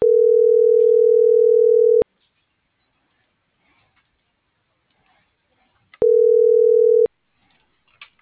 Ringback tone (North America), 440 + 480 Hz.